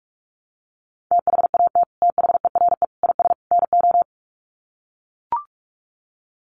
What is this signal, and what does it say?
Morse code, heard as a keyed tone, about 44 wpm, 700 Hz. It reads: THAT THERE IS NO
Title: Sets of 4 Words 45wpm